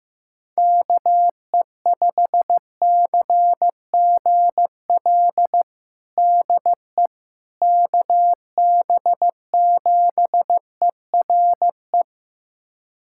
Morse code, heard as a keyed tone, about 15 wpm, 700 Hz. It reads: KE5CGL DE KB7ERE
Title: USA Callsign DE Callsign - 6 Characters - 15wpm